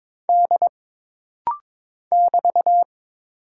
Morse code, heard as a keyed tone, about 22 wpm, 700 Hz. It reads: D <BT>
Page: This rapid-fire timing is geared to push your ICR (Instant Character Recognition) skills.